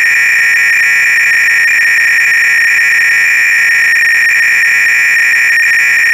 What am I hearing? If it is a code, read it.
300-baud modem data.
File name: Repository / ACR-Audio-Cassette-Programs / MITS-BASIC-GAME-LUNAR-LANDER-CLOADL